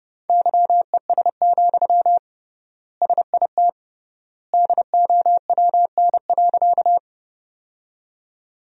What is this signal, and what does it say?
Morse code, heard as a keyed tone, about 30 wpm, 700 Hz. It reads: YES, SIT DOWN.
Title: 3x Word-Spacing - Sentences from Top 500 Words - 30wpm